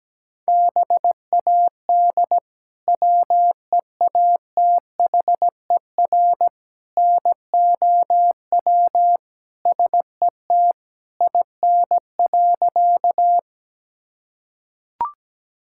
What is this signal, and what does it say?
Morse code, heard as a keyed tone, about 17 wpm, 700 Hz. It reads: BAD WEATHER NOW SET IN.
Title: Sentences from Top 1000 Words - Review Part 5 - 17wpm